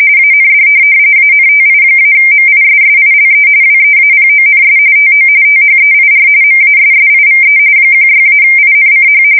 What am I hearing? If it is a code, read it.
RTTY, about 45.45 baud. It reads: AND ANTI-MISSILE WEAPONS TO THEIR MASSIVE 16-INCH
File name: fsk170 / USS-New-Jersey